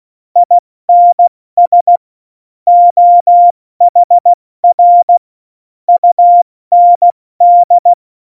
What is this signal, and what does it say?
Morse code, heard as a keyed tone, about 16 wpm, spaced as for 14 wpm, 700 Hz. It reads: INS OHR UND
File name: Morsekurs / Witze / Witz 13 March 2018, 70 F80